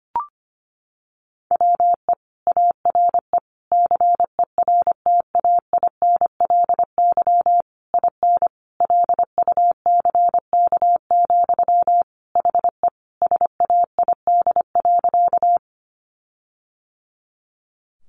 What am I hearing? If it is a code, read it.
Morse code, heard as a keyed tone, about 25 wpm, 700 Hz. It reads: WE ARE CERTAINLY IN LUCK, HE SAID.
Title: Sentences from Top 1500 Words 25wpm